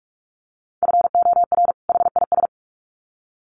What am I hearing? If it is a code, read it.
Morse code, heard as a keyed tone, about 44 wpm, 700 Hz. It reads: FOR HIS
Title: Sets of 2 Words 45wpm